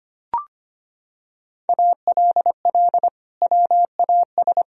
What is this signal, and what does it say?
Morse code, heard as a keyed tone, about 25 wpm, 700 Hz. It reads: ALL WAS
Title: Sentences from Top 1000 Words - Review Part 2 - 25wpm